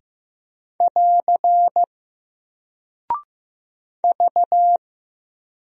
Morse code, heard as a keyed tone, about 15 wpm, 700 Hz. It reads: <AR> V
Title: Single Character (Letter:Number:Punctuation:Prosign) Alt 15wpm